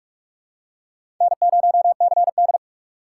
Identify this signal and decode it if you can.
Morse code, heard as a keyed tone, about 44 wpm, 700 Hz. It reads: N0CD
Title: USA Call Signs - 6x Character Spacing - 45wpm